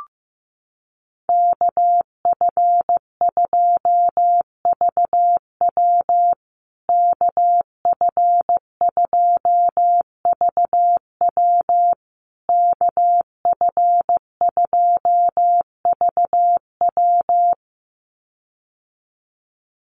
Morse code, heard as a keyed tone, about 15 wpm, 700 Hz. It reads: KF2VW KF2VW KF2VW
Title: USA Call Signs Repeated 3x 15wpm